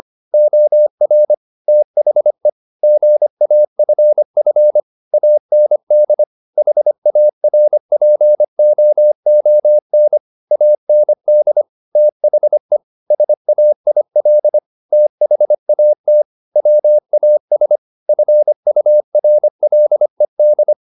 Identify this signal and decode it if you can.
Morse code, heard as a keyed tone, about 25 wpm, 600 Hz. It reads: OR THE GAFF AND HARPOON AND THE SAIL THAT WAS FURLED